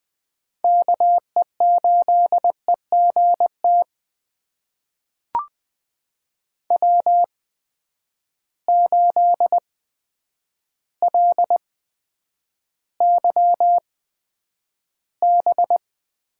Morse code, heard as a keyed tone, about 20 wpm, spaced as for 9 wpm, 700 Hz. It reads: KE8EGT W8LYB
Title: USA Call Signs - 8x Character Spacing - 20wpm